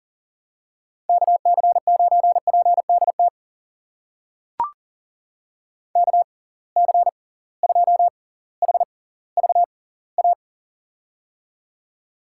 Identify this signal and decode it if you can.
Morse code, heard as a keyed tone, about 41 wpm, spaced as for 21 wpm, 700 Hz. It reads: KC9PDT KC2HVA
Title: USA Call Signs - 6 Characters - 6x Character Spacing - 40wpm